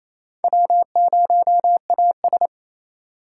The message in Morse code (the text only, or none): W0AS